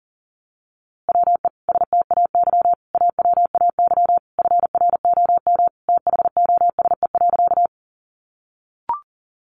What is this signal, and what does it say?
Morse code, heard as a keyed tone, about 40 wpm, 700 Hz. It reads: WE STAY AWAY FROM THOSE.
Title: A Sound Of Thunder 40wpm